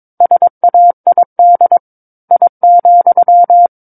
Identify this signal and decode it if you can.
Morse code, heard as a keyed tone, about 22 wpm, 700 Hz. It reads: SAID I,
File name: Book One / 13-Chapter 13 - 22 WPM